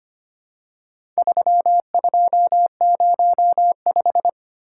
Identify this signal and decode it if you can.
Morse code, heard as a keyed tone, about 25 wpm, 700 Hz. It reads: 3205
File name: practice / 25wpm / Four Letter-Number Combinations - 25wpm